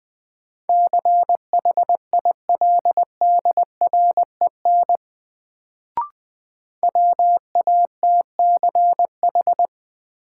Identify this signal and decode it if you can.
Morse code, heard as a keyed tone, about 21 wpm, 700 Hz. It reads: CHILDREN WATCH
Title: Sentences from Top 300 Words - New Word Drill - 20wpm